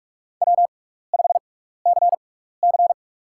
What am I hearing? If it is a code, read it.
Morse code, heard as a keyed tone, about 47 wpm, spaced as for 16 wpm, 700 Hz. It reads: W5CC
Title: USA Call Signs - 6x Character Spacing - 45wpm